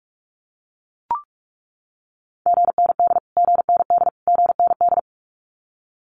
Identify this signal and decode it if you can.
Morse code, heard as a keyed tone, about 44 wpm, 700 Hz. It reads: GND GND GND
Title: Abbreviations Repeated 3x - 45wpm